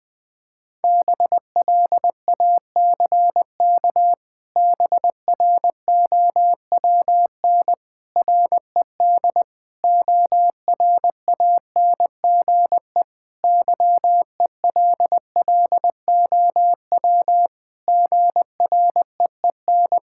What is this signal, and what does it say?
Morse code, heard as a keyed tone, about 20 wpm, 700 Hz. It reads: BLACK BROWN RED ORANGE YELLOW GREEN
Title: Pangram Challenge v2 - 20wpm